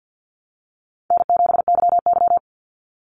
Morse code, heard as a keyed tone, about 50 wpm, 700 Hz. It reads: N7YY
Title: Easy Call Signs - 50wpm